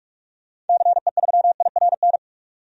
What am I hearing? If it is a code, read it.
Morse code, heard as a keyed tone, about 46 wpm, 700 Hz. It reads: CE3IRN
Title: DXCC-100 Call Signs - 6 Characters - 45wpm